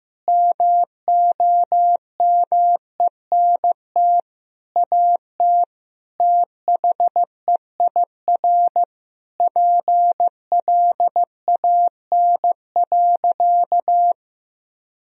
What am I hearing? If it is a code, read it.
Morse code, heard as a keyed tone, about 15 wpm, 700 Hz. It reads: MOMENT AT THEIR PLAN.
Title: Sentences from Top 1000 Words - Review Part 2 - 15wpm